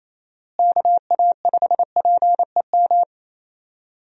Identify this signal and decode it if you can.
Morse code, heard as a keyed tone, about 27 wpm, 700 Hz. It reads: KA5PEM
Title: USA Call Signs - 6 Characters - Repeated 2x - 28wpm